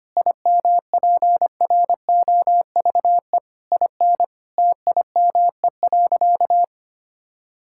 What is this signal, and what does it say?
Morse code, heard as a keyed tone, about 25 wpm, 700 Hz. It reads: IMPROVE IN TIME.
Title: Sentences from Top 1000 Words - Review Part 5 - 25wpm